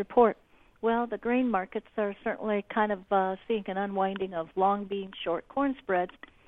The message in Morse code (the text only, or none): none